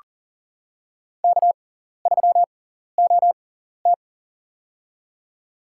Morse code, heard as a keyed tone, about 41 wpm, spaced as for 14 wpm, 700 Hz. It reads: K3OT
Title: USA Call Signs - 4 Characters - 6x Character Spacing - 40wpm